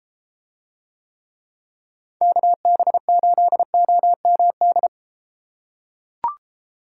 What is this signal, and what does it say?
Morse code, heard as a keyed tone, about 33 wpm, 700 Hz. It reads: KB8OMD